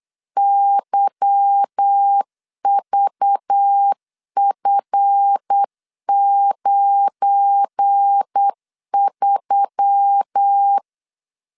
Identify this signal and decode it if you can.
Morse code, heard as a keyed tone, about 8 wpm, 800 Hz. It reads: YVF93
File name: LEZIONI / 8 YJ45 / 8 Y 8wpm 3245